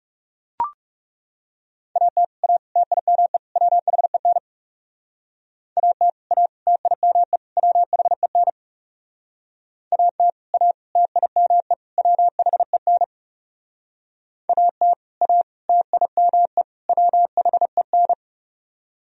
Morse code, heard as a keyed tone, about 37 wpm, 700 Hz. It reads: AT A TIME WHEN AT A TIME WHEN AT A TIME WHEN AT A TIME WHEN
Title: Speed-Racing - Sets of 4 Words 30wpm